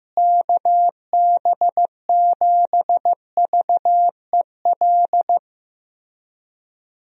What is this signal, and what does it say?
Morse code, heard as a keyed tone, about 15 wpm, 700 Hz. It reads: KB7VEL
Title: USA Call Signs - 6 Characters - Repeated 3x - 15wpm